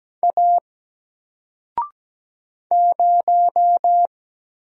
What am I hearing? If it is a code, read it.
Morse code, heard as a keyed tone, about 17 wpm, 700 Hz. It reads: A 0
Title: Single Character (Letter:Number:Punctuation:Prosign) Alt 17wpm